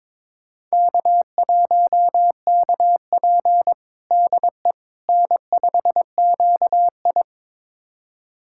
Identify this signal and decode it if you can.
Morse code, heard as a keyed tone, about 22 wpm, 700 Hz. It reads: K1KP DE N5QI
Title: USA Callsign DE Callsign - 4 Characters - 22wpm